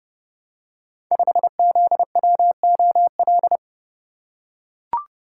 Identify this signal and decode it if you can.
Morse code, heard as a keyed tone, about 30 wpm, 700 Hz. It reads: 5ZWOL
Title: Code Groups - 5 Letter:Numbers - 30wpm